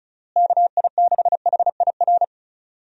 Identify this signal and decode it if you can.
Morse code, heard as a keyed tone, about 34 wpm, 700 Hz. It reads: KI6HIR
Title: USA Call Signs - 6 Characters - Repeated 3x - 35wpm